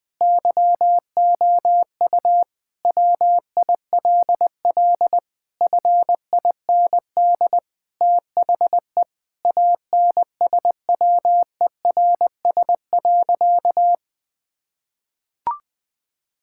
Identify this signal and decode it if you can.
Morse code, heard as a keyed tone, about 20 wpm, 700 Hz. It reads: YOU WILL FIND THE ANSWERS.
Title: Sentences with 5-Word Phrases - 20wpm